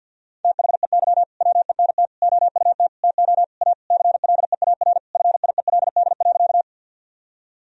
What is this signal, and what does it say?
Morse code, heard as a keyed tone, about 53 wpm, 700 Hz. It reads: THEY WENT OUT TO A CLEAR FIELD.
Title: Sentences from Top 1000 Words - Review Part 2 - 50wpm